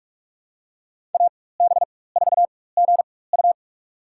Morse code, heard as a keyed tone, about 46 wpm, spaced as for 21 wpm, 700 Hz. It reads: AB4GU
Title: USA Call Signs - 4x Character Spacing - 45wpm